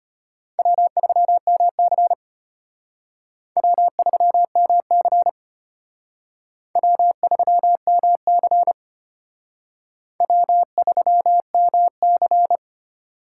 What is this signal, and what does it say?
Morse code, heard as a keyed tone, about 33 wpm, 700 Hz. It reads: W3MC W3MC W3MC W3MC